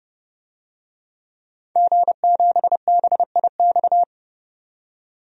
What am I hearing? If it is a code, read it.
Morse code, heard as a keyed tone, about 30 wpm, 700 Hz. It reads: G7BIX
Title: DXCC-100 Call Signs - 5 Characters - 30wpm